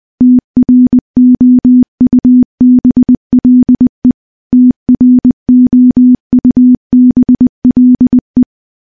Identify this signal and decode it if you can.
Morse code, heard as a keyed tone, about 19 wpm, 250 Hz. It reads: TROUBLE TROUBLE
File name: morse-training-20WPM